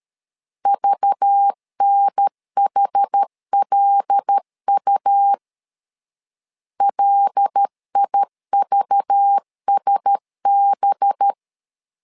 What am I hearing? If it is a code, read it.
Morse code, heard as a keyed tone, about 13 wpm, 800 Hz. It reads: VNHLU LIVSB
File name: LEZIONI / 5 LBVC / 5 LBV 12wpm 1855